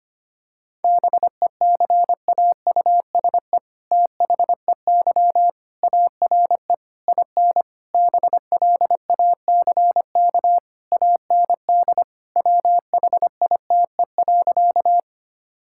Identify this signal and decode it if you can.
Morse code, heard as a keyed tone, about 25 wpm, 700 Hz. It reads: BECAUSE THEY ARE IN BLACK AND WHITE.
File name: practice / 25wpm / All-Age Appropriate Jokes - 25wpm